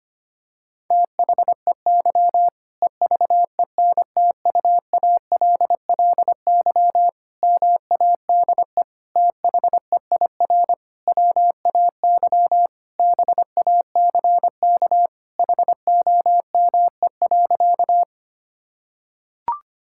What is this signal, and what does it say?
Morse code, heard as a keyed tone, about 25 wpm, 700 Hz. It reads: THEY EVENTUALLY MADE THEIR WAY BACK HOME.
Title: Sentences from Top 1000 Words - Review Part 6 - 25wpm